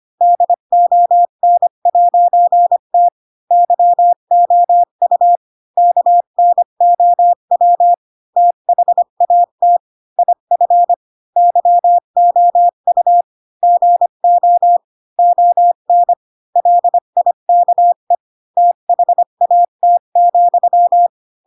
Morse code, heard as a keyed tone, about 25 wpm, 700 Hz. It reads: DON'T YOU KNOW THAT IF YOU GO ON LIKE THAT,
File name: Pinocchio / Pinocchio ch4 20 25 ik1hge